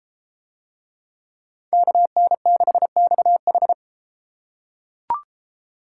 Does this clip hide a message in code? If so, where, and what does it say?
Yes: Morse code from 1.7 to 3.7 s, heard as a keyed tone, about 29 wpm, 700 Hz. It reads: KN6XH